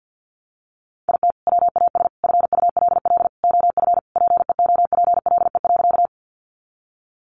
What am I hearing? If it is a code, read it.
Morse code, heard as a keyed tone, about 50 wpm, 700 Hz. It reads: IT WAS FULL OF PEOPLE.
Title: Sentences from Top 500 Words 50wpm